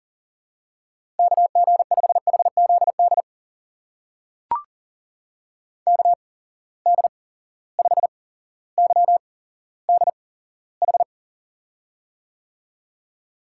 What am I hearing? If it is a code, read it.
Morse code, heard as a keyed tone, about 39 wpm, spaced as for 17 wpm, 700 Hz. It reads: KG5HZD KD5YDH
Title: USA Call Signs - 6 Characters - 8x Character Spacing - 40wpm